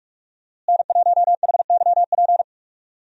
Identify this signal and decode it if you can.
Morse code, heard as a keyed tone, about 42 wpm, 700 Hz. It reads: N1HYP
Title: USA Call Signs - 5 Characters - Repeated 2x - 45wpm